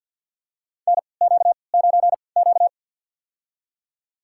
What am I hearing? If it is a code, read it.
Morse code, heard as a keyed tone, about 47 wpm, spaced as for 28 wpm, 700 Hz. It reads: NQ9Q
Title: USA Call Signs - 4 Characters - 3x Character Spacing - 50wpm